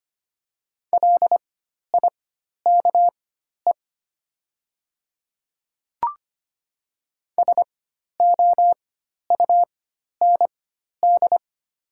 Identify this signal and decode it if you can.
Morse code, heard as a keyed tone, about 25 wpm, spaced as for 12 wpm, 700 Hz. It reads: LIKE SOUND
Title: 4x Character Spacing - Top 200 Words - 25wpm